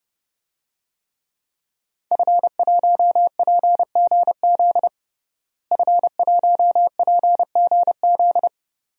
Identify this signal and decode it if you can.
Morse code, heard as a keyed tone, about 30 wpm, 700 Hz. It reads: F1PGZ F1PGZ
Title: DXCC-100 Call Signs - 5 Characters - Repeated 2x - 30wpm